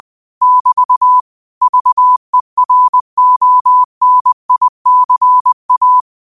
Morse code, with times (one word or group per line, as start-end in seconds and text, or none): 0.4-1.2 <BT>
1.6-6.0 VERONICA